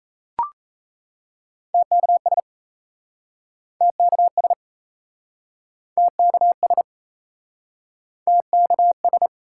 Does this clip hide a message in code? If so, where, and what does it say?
Yes: Morse code from 1.7 to 9.3 s, heard as a keyed tone, about 39 wpm, 700 Hz. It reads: TKS TKS TKS TKS